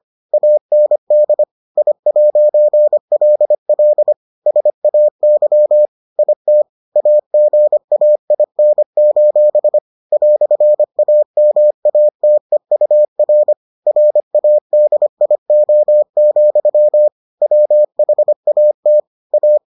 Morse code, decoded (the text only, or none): AND I'LL SAY IT AGAIN: "AMATEUR RADIO, WHAT A